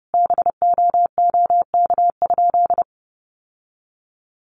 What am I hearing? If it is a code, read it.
Morse code, heard as a keyed tone, about 30 wpm, 700 Hz. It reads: BOOK?
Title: Sentences from Top 1400 Words 30wpm